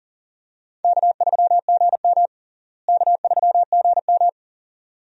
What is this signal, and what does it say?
Morse code, heard as a keyed tone, about 40 wpm, 700 Hz. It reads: K3GM K3GM
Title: USA Call Signs - 4 Characters - Repeated 2x - 40wpm